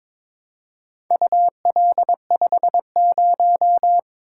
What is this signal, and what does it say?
Morse code, heard as a keyed tone, about 22 wpm, 700 Hz. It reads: UL50
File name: practice / 22wpm / Four Letter-Number Combinations - 22wpm